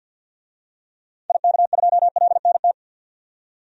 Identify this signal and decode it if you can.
Morse code, heard as a keyed tone, about 52 wpm, 700 Hz. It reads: IK2LNT
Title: DXCC-100 Call Signs - 6 Characters - 50wpm